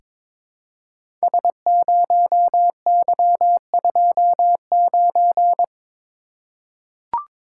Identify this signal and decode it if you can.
Morse code, heard as a keyed tone, about 22 wpm, 690 Hz. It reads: S0Y29
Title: Code Groups - 5 Letter:Numbers - 22wpm